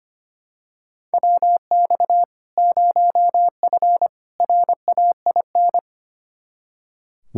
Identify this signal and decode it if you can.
Morse code, heard as a keyed tone, about 25 wpm, 700 Hz. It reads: WX 0F RAIN
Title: Practice QSO - Exchange 2 - Info - 25wpm